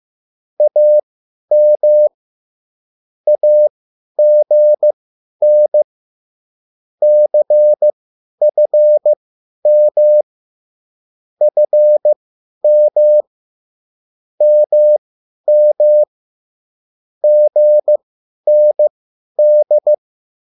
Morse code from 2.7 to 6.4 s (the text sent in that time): AGN